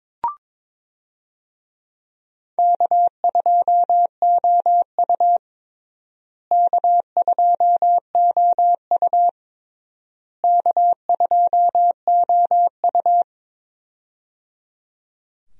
Morse code, heard as a keyed tone, about 22 wpm, 700 Hz. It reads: K2OU K2OU K2OU
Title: USA Call Signs - 4 Characters - Repeated 3x - 22wpm